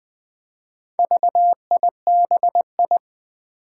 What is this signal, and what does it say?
Morse code, heard as a keyed tone, about 20 wpm, 700 Hz. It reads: VIBI
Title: Four Challenging Letter-Number Combinations 20wpm